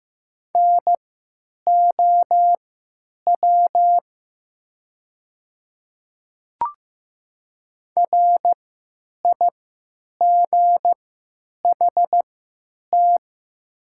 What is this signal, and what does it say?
Morse code, heard as a keyed tone, about 15 wpm, spaced as for 9 wpm, 700 Hz. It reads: NOW RIGHT
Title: 3x Character Spacing - Top 200 Words - 15wpm